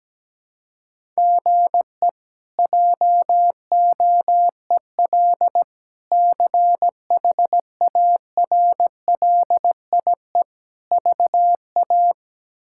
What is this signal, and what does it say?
Morse code, heard as a keyed tone, about 17 wpm, 700 Hz. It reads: GE JOEL CHARLIE VA
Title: K1USN SST QSO Practice - 17wpm